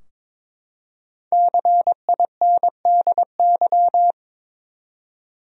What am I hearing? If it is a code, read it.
Morse code, heard as a keyed tone, about 22 wpm, 700 Hz. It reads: CINDY
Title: USA Common Names - 22wpm